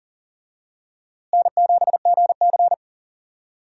Morse code, heard as a keyed tone, about 40 wpm, 700 Hz. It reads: N7GC
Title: USA Call Signs - 4 Characters - Repeated 2x - 40wpm